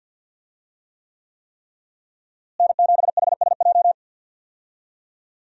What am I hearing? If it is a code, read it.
Morse code, heard as a keyed tone, about 50 wpm, 700 Hz. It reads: N7HSJ
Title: USA Call Signs - 5 Characters - Repeated 3x - 50wpm